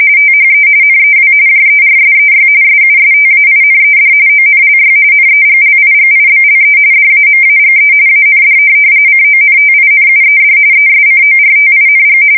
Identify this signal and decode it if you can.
RTTY, about 45.45 baud. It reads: THE NEW JERSEY SUSTAINED ONLY NEGLIGIBLE DAMAGE AND CONTINUED ESCORTING
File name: fsk170 / USS-New-Jersey